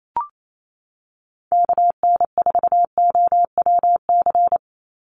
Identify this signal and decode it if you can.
Morse code, heard as a keyed tone, about 28 wpm, 700 Hz. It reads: KN4OWC
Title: USA Call Signs 28wpm